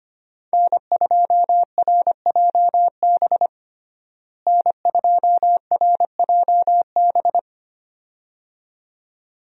Morse code, heard as a keyed tone, about 25 wpm, 700 Hz. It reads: N2RJB N2RJB
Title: USA Call Signs - 5 Characters - Repeated 2x - 25wpm